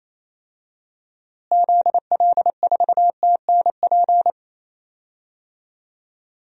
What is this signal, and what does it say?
Morse code, heard as a keyed tone, about 28 wpm, 700 Hz. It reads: ZL4TNP
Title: DXCC-100 Call Signs - 6 Characters - Repeated 3x - 28wpm